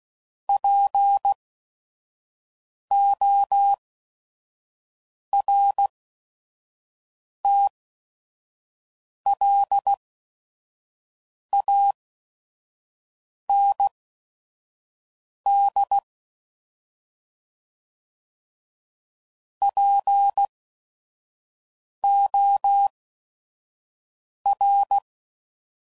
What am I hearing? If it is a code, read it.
Morse code, heard as a keyed tone, about 16 wpm, spaced as for 5 wpm, 800 Hz. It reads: PORTLAND POR
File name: CW Code Groups @5WPM / fromSM's / Cities